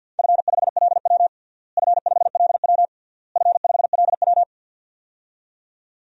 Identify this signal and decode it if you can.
Morse code, heard as a keyed tone, about 49 wpm, 700 Hz. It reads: F5LW F5LW F5LW
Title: DXCC-100 Call Signs - 4 Characters - Repeated 3x - 50wpm